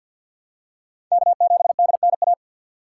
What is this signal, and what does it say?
Morse code, heard as a keyed tone, about 53 wpm, 700 Hz. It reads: K7DNA
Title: USA Call Signs - 5 Characters - 4x Character Spacing - 50wpm